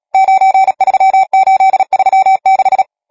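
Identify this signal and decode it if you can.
Morse code, heard as a keyed tone, about 37 wpm, 770 Hz. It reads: 93836
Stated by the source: Morse code number practice, machine-sent training text (Nácvik telegrafie, OK1FCB).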